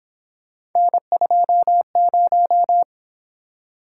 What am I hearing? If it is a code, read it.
Morse code, heard as a keyed tone, about 26 wpm, 700 Hz. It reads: N20
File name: Speed-Racing - Three Letter-Number Combinations - 22wpm